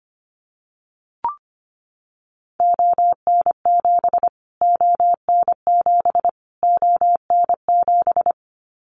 Morse code, heard as a keyed tone, about 25 wpm, 700 Hz. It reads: ON7 ON7 ON7
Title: Callsign Prefixes for Top 25 DXCC - Repeated 3x - 25wpm